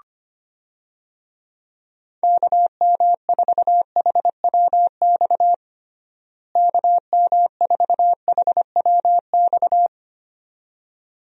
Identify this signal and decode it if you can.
Morse code, heard as a keyed tone, about 25 wpm, 700 Hz. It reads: KM4HWX KM4HWX
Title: USA Call Signs - 6 Characters - Repeated 2x - 25wpm